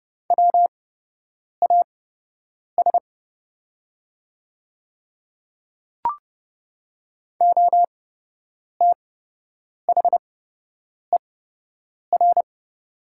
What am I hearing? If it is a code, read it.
Morse code, heard as a keyed tone, about 30 wpm, spaced as for 8 wpm, 700 Hz. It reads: WAS OTHER
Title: 8x Character Spacing - Top 100 Words - 30wpm